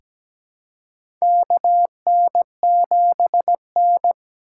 Morse code, heard as a keyed tone, about 17 wpm, 700 Hz. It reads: KN7N
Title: USA Call Signs - 4 Characters - 2x Character Spacing - 17wpm